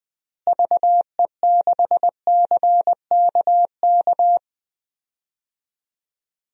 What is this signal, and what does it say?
Morse code, heard as a keyed tone, about 20 wpm, 700 Hz. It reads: VE6CKK
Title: DXCC-100 Call Signs - 6 Characters - Repeated 3x - 20wpm